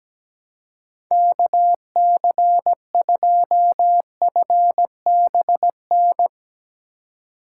Morse code, heard as a keyed tone, about 18 wpm, 700 Hz. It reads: KC2FBN
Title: Speed Racing - USA 2x3 Call Signs 17wpm